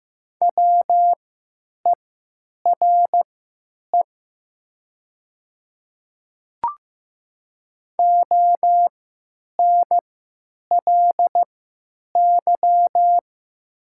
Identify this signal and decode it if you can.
Morse code, heard as a keyed tone, about 15 wpm, spaced as for 9 wpm, 700 Hz. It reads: WERE ONLY
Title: 3x Character Spacing - Top 500 Words - 15wpm